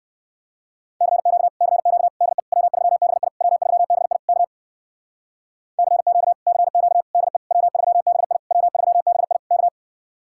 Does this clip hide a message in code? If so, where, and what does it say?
Yes: Morse code from 1.0 to 9.7 s, heard as a keyed tone, about 59 wpm, 700 Hz. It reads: CQ CQ DE W3BI W3BI K CQ CQ DE W3BI W3BI K